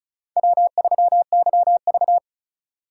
Morse code, heard as a keyed tone, about 46 wpm, 700 Hz. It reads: W3YV